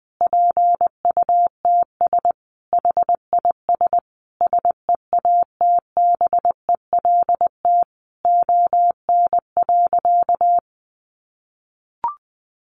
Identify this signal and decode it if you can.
Morse code, heard as a keyed tone, about 20 wpm, 700 Hz. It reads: PUTS HIS SEATBELT ON.
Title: Additional Easy Phrases 20wpm